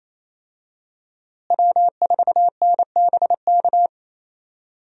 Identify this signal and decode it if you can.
Morse code, heard as a keyed tone, about 28 wpm, 700 Hz. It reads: W4NBK
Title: USA Call Signs - 5 Characters - Repeated 2x - 28wpm